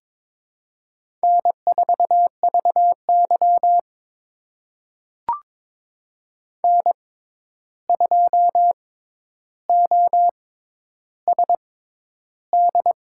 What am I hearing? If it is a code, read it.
Morse code, heard as a keyed tone, about 22 wpm, spaced as for 10 wpm, 700 Hz. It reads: N4VY N2OSD
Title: USA Call Signs - 6x Character Spacing - 22wpm